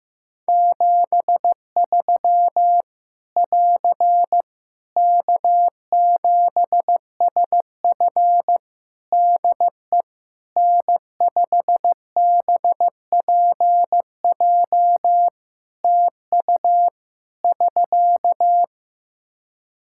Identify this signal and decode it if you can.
Morse code, heard as a keyed tone, about 16 wpm, 700 Hz. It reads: 73 <AR> K7SF DE N5BPJ TU <SK>
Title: Practice QSO - Exchange 4 - Ending - 15wpm